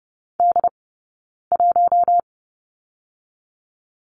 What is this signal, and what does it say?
Morse code, heard as a keyed tone, about 30 wpm, 700 Hz. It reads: D 1
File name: 092 Lesson 16d - Letter c - Two Characters 30wpm